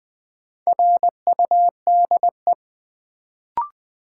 Morse code, heard as a keyed tone, about 20 wpm, 700 Hz. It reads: RUDE
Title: Build A Word - 2 Letters At A Time - 4-Letter Words - 20wpm